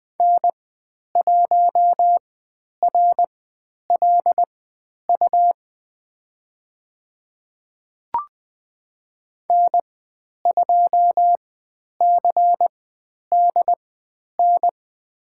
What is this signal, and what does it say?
Morse code, heard as a keyed tone, about 20 wpm, spaced as for 10 wpm, 700 Hz. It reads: N1RLU N2CDN
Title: 102 Lesson 17g - Letter u - Call Signs 20wpm